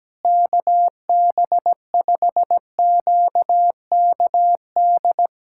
Morse code, heard as a keyed tone, about 17 wpm, 700 Hz. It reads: KB5QKD
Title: USA Callsign DE Callsign - 6 Characters - 17wpm